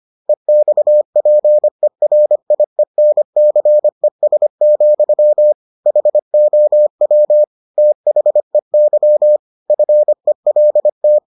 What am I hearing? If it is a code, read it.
Morse code, heard as a keyed tone, about 25 wpm, 600 Hz. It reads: EXPERIENCES, HOW THEY FELT